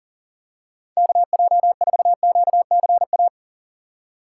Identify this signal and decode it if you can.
Morse code, heard as a keyed tone, about 40 wpm, 700 Hz. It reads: KJ4QCA
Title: USA Call Signs - 6 Characters - 3x Character Spacing - 40wpm